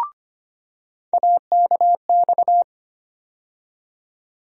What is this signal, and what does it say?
Morse code, heard as a keyed tone, about 25 wpm, 700 Hz. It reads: AKX
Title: USA Call Sign Postfixes - 3 Letters 25wpm